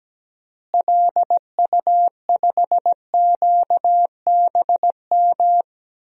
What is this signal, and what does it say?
Morse code, heard as a keyed tone, about 17 wpm, 700 Hz. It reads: LU5QBM
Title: DXCC-100 Call Signs - 6 Characters - Repeated 2x - 17wpm